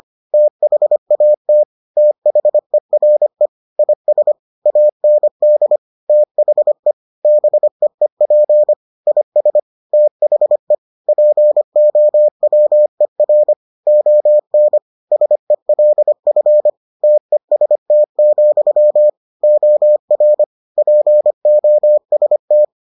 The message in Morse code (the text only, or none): THAT THERE IS AND THE BEEP IS THE POWER ON SELF TEST, OR POST